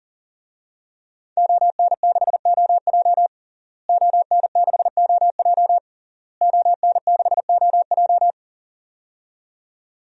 Morse code, heard as a keyed tone, about 40 wpm, 700 Hz. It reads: ON6OJ ON6OJ ON6OJ
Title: DXCC-100 Call Signs - 5 Characters - Repeated 3x - 40wpm